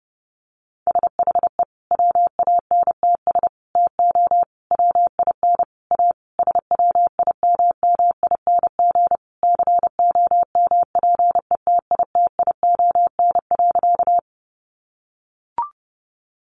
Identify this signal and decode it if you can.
Morse code, heard as a keyed tone, about 30 wpm, 700 Hz. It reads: SHE WANTS TO WIN A SWIMMING COMPETITION.
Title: Additional Easy Phrases 30wpm